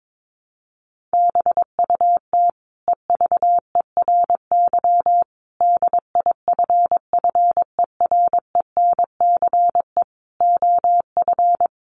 Morse code, heard as a keyed tone, about 22 wpm, 700 Hz. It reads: BUT EVERY DIFFERENCE OF
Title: Sentences from Top 1400 Words 22wpm